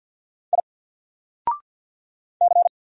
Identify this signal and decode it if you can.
Morse code, heard as a keyed tone, about 48 wpm, 700 Hz. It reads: I C
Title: Single Letter-Number Alt 50wpm